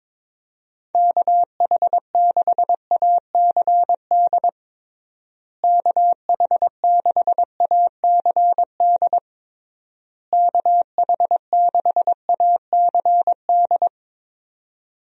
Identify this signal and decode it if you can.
Morse code, heard as a keyed tone, about 22 wpm, 700 Hz. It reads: KH6ACD KH6ACD KH6ACD
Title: USA Call Signs - 6 Characters - Repeated 3x - 22wpm